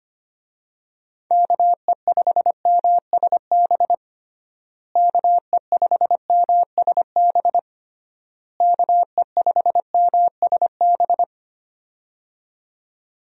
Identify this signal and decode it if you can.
Morse code, heard as a keyed tone, about 25 wpm, 700 Hz. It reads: KE5MSB KE5MSB KE5MSB
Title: USA Call Signs - 6 Characters - Repeated 3x - 25wpm